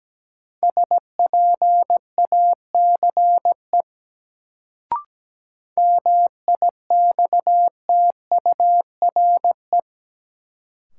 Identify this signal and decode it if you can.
Morse code, heard as a keyed tone, about 17 wpm, 700 Hz. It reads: SPACE MIXTURE
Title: Sentences from Top 1500 Words - New Word Drill Alt - 17wpm